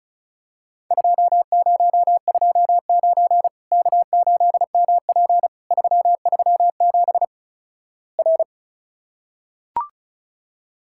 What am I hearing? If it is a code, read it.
Morse code, heard as a keyed tone, about 36 wpm, 700 Hz. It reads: 2029 K8MP 337 R
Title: Worked All Europe DX Contest - QTC Exchange - 35wpm